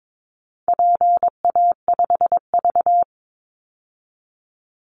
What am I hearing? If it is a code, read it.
Morse code, heard as a keyed tone, about 22 wpm, 700 Hz. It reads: PA5V
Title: Call Signs Repeated 3x - 22wpm